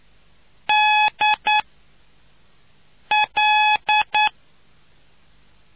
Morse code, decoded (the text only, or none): D L